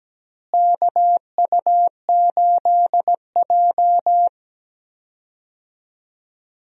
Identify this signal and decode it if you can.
Morse code, heard as a keyed tone, about 17 wpm, 700 Hz. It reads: KU8J
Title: USA Call Signs - 4 Characters - Repeated 2x - 17wpm